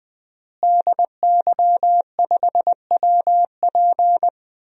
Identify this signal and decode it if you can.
Morse code, heard as a keyed tone, about 20 wpm, 700 Hz. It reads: DY5WP
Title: DXCC-100 Call Signs - 5 Characters - Repeated 3x - 20wpm